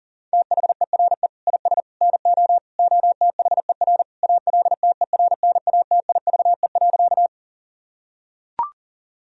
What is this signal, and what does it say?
Morse code, heard as a keyed tone, about 40 wpm, 700 Hz. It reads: THERE IS NO OTHER ALTERNATIVE.
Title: Sentences from Top 1000 Words - Review Part 4 - 40wpm